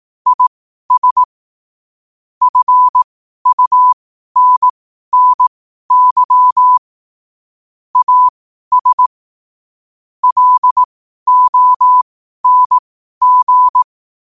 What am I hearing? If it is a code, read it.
Morse code, heard as a keyed tone, about 18 wpm, spaced as for 12 wpm, 1000 Hz. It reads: IS FUNNY AS LONG
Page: New quotes every day in morse code at 12 Words per minute.